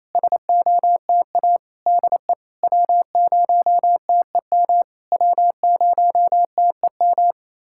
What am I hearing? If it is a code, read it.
Morse code, heard as a keyed tone, about 27 wpm, 690 Hz. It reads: SOTA DE W0TEM W0TEM
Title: SOTA QSO Practice 28wpm